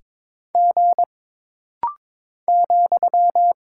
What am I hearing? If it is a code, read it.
Morse code, heard as a keyed tone, about 22 wpm, 700 Hz. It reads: G ,
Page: This rapid-fire timing is geared to push your ICR (Instant Character Recognition) skills.